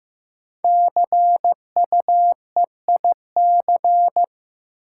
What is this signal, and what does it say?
Morse code, heard as a keyed tone, about 16 wpm, 700 Hz. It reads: CUEIC
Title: Code Groups - 5 Letter:Numbers - 15wpm